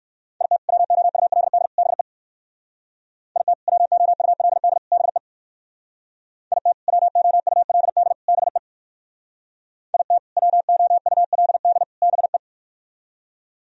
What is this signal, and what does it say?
Morse code, heard as a keyed tone, about 57 wpm, 700 Hz. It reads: IT WOULD BE IT WOULD BE IT WOULD BE IT WOULD BE